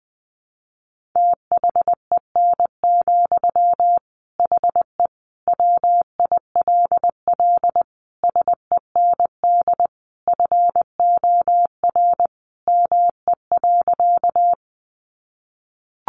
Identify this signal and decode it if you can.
Morse code, heard as a keyed tone, about 20 wpm, 700 Hz. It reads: THEN, HE WILL SEND FOR ME.
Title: Sentences from Top 1500 Words 20wpm